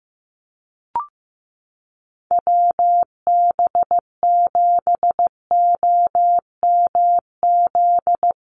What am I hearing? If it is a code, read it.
Morse code, heard as a keyed tone, about 15 wpm, 700 Hz. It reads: WB7OMZ
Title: USA Call Signs Repeated 2x 15wpm